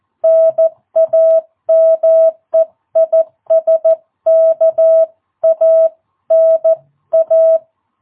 Morse code, heard as a keyed tone, about 14 wpm, 660 Hz. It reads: NAME IS KANA
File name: jh8cbh / 2022cw